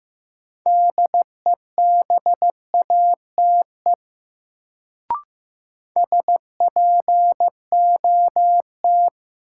Morse code, heard as a keyed tone, about 15 wpm, 700 Hz. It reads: DEBATE SPOT
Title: Sentences from Top 1100 Words - New Word Drill Alt - 15wpm